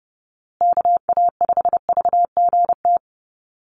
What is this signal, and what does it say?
Morse code, heard as a keyed tone, about 30 wpm, 700 Hz. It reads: KA5VGT
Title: USA Call Signs Repeated 3x 30wpm